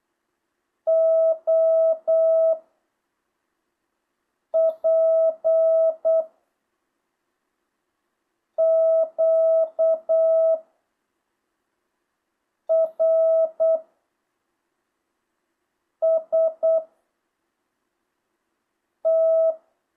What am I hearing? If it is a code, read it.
Morse code, heard as a keyed tone, about 8 wpm, 650 Hz. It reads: O P Q R S T
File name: jh8cbh